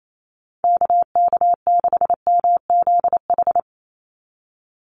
Morse code, heard as a keyed tone, about 28 wpm, 700 Hz. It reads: KK6MZH
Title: USA Call Signs 28wpm